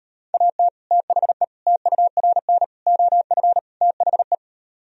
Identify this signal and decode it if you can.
Morse code, heard as a keyed tone, about 38 wpm, 700 Hz. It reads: AT THE TURN OF THE